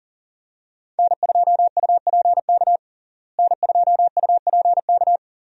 Morse code, heard as a keyed tone, about 40 wpm, 700 Hz. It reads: N2UPK N2UPK
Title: USA Call Signs - 5 Characters - Repeated 3x - 40wpm